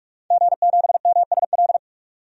Morse code, heard as a keyed tone, about 46 wpm, 700 Hz. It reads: G7MSL
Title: DXCC-100 Call Signs - 5 Characters - 45wpm